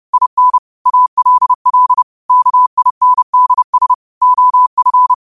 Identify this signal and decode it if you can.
Morse code, heard as a keyed tone, about 30 wpm, 1000 Hz. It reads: IN ALL KINDS OF
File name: qotdfiles / 30wpm